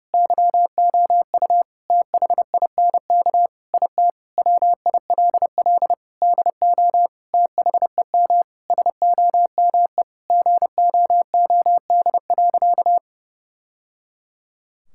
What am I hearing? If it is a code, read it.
Morse code, heard as a keyed tone, about 30 wpm, 700 Hz. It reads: YOU THINK IT WILL DO THEM SOME GOOD.
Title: Sentences from Top 1000 Words - Review Part 6 - 30wpm